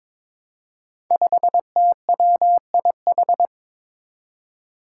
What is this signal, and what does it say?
Morse code, heard as a keyed tone, about 22 wpm, 700 Hz. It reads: 5TWIH
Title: Five Challenging Letter-Number Combinations 22wpm